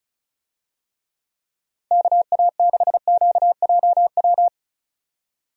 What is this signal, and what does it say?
Morse code, heard as a keyed tone, about 34 wpm, 700 Hz. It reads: KA6QJW
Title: USA Call Signs - 6 Characters - 35wpm